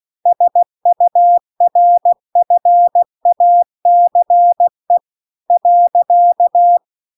Morse code, heard as a keyed tone, about 16 wpm, 700 Hz. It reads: SURFACE .
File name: CW-The-Big-Twoheaded-River-2-16WPM